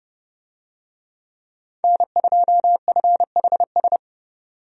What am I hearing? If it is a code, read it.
Morse code, heard as a keyed tone, about 32 wpm, 700 Hz. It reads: N2FHS
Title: USA Call Signs - 5 Characters - 30wpm